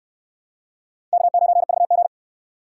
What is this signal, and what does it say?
Morse code, heard as a keyed tone, about 63 wpm, 700 Hz. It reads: K9VG